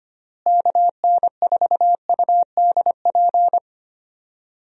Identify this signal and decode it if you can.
Morse code, heard as a keyed tone, about 25 wpm, 700 Hz. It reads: KN4UDP
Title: USA Call Signs - 6 Characters - Repeated 3x - 25wpm